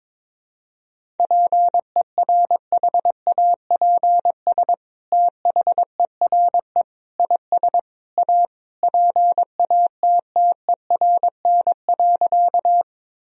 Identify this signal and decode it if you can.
Morse code, heard as a keyed tone, about 22 wpm, 700 Hz. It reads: PERHAPS THERE IS A PATTERN.
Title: Sentences from Top 1000 Words - Review Part 2 - 22wpm